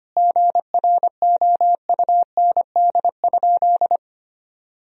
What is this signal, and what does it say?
Morse code, heard as a keyed tone, about 25 wpm, 700 Hz. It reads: GROUND?
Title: Sentences from Top 1000 Words - Review Part 2 - 25wpm